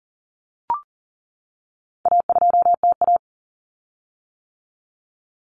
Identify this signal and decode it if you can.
Morse code, heard as a keyed tone, about 40 wpm, 700 Hz. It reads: A2TA
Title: Common Cut Numbers - 4-Digit 40wpm